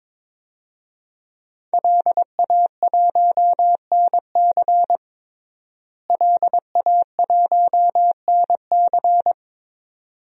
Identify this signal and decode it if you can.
Morse code, heard as a keyed tone, about 22 wpm, 700 Hz. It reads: LA1NC LA1NC
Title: DXCC-100 Call Signs - 5 Characters - Repeated 2x - 22wpm